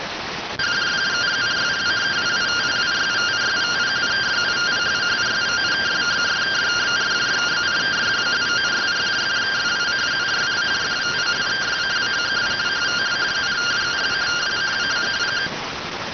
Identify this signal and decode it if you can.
Two-tone teleprinter signal, about 170 Hz shift.